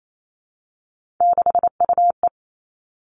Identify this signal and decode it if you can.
Morse code, heard as a keyed tone, about 28 wpm, 700 Hz. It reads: 6UE
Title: Three Challenging Letter-Number Combinations 28wpm